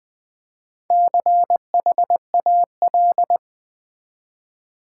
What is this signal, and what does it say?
Morse code, heard as a keyed tone, about 20 wpm, 700 Hz. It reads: CHAL
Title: Build A Word - 4 Letters At A Time - 11-Letter Words - 20wpm